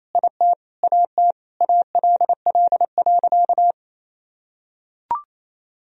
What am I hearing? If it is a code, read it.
Morse code, heard as a keyed tone, about 28 wpm, 700 Hz. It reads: IT AT ALL.
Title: Sentences from Top 1000 Words - Review Part 1 - 28wpm